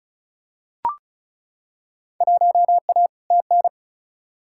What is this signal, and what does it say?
Morse code, heard as a keyed tone, about 36 wpm, 700 Hz. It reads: 1A TN
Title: Field Day Exchanges - 35wpm